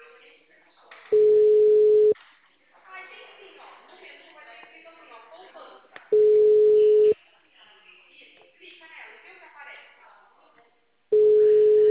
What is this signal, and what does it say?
Ringback tone (Europe, 425 Hz), with speech.